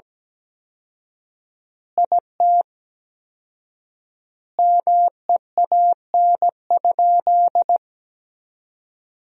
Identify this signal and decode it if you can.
Morse code, heard as a keyed tone, about 17 wpm, 700 Hz. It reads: IT MEAN?
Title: 4x Word-Spacing - Sentences from Top 200 Words - 17wpm